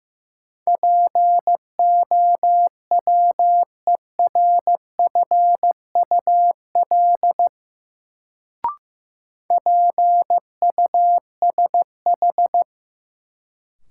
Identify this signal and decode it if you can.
Morse code, heard as a keyed tone, about 15 wpm, 700 Hz. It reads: POWERFUL PUSH
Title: Sentences from Top 1300 Words - New Word Drill Alt - 15wpm